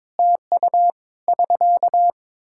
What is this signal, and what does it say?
Morse code, heard as a keyed tone, about 22 wpm, 700 Hz. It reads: TU <SK>
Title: Practice QSO - Exchange 4 - Ending - 22wpm